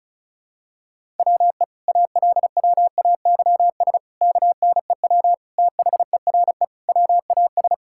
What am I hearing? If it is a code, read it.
Morse code, heard as a keyed tone, about 36 wpm, 700 Hz. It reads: WE ALWAYS KNEW THERE WAS
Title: Sentences from Top 1000 Words - Review Part 6 - 35wpm